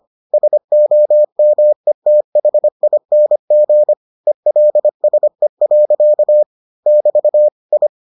Morse code, heard as a keyed tone, about 25 wpm, 600 Hz. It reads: SOMETHING ELSE. <BT> I